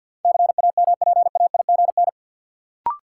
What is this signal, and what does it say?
Morse code, heard as a keyed tone, about 53 wpm, 700 Hz. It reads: CAMPAIGN
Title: Sentences from Top 1200 Words - New Word Drill Alt - 50wpm